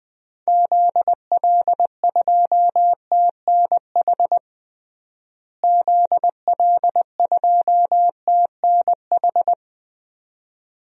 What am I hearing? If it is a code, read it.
Morse code, heard as a keyed tone, about 20 wpm, 700 Hz. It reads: ZL2TNH ZL2TNH
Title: DXCC-100 Call Signs - 6 Characters - Repeated 2x - 20wpm